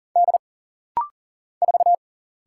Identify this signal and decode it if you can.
Morse code, heard as a keyed tone, about 39 wpm, 700 Hz. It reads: D 4
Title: Single Letter-Number - Rapid-Fire 40wpm